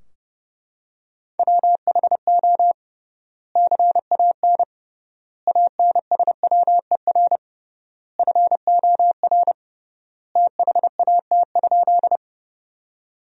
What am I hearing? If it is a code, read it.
Morse code, heard as a keyed tone, about 30 wpm, 700 Hz. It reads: WHO CAN ANSWER FOR THAT?
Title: 3x Word-Spacing - Sentences from Top 300 Words - 30wpm